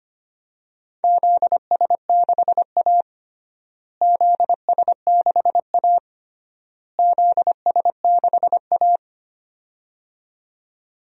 Morse code, heard as a keyed tone, about 25 wpm, 700 Hz. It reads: ZS6A ZS6A ZS6A
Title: DXCC-100 Call Signs - 4 Characters - Repeated 3x - 25wpm